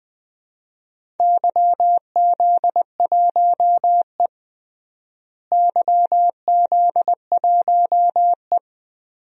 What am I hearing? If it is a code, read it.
Morse code, heard as a keyed tone, about 21 wpm, 700 Hz. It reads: YZ1E YZ1E
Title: DXCC-100 Call Signs - 4 Characters - Repeated 3x - 20wpm